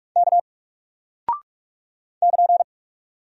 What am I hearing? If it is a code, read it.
Morse code, heard as a keyed tone, about 46 wpm, 700 Hz. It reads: K <KN>
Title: Single Character (Letter:Number:Punctuation:Prosign) Alt 45wpm